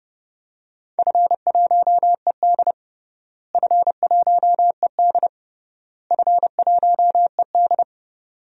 Morse code, heard as a keyed tone, about 30 wpm, 700 Hz. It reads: F1ED F1ED F1ED
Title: DXCC-100 Call Signs - 4 Characters - Repeated 3x - 30wpm